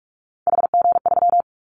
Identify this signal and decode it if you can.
Morse code, heard as a keyed tone, about 44 wpm, 700 Hz. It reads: HG3
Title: Callsign Prefixes for Top 50 DXCC - 45wpm